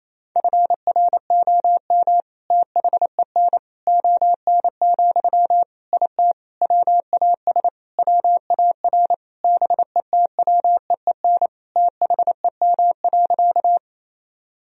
Morse code, heard as a keyed tone, about 28 wpm, 700 Hz. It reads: FROM THEN ON, IT WAS WAR BETWEEN THEM.
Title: Sentences from Top 1000 Words - Review Part 2 - 28wpm